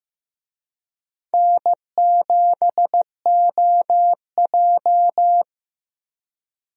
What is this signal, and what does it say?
Morse code, heard as a keyed tone, about 15 wpm, 700 Hz. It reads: N7OJ
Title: USA Call Signs - 4 Characters - Repeated 2x - 15wpm